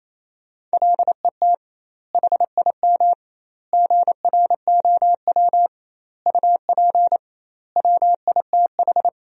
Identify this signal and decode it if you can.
Morse code, heard as a keyed tone, about 27 wpm, 700 Hz. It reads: LET HIM GROW UP WITH
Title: 2x Word-Spacing - Sentences from Top 300 Words - 28wpm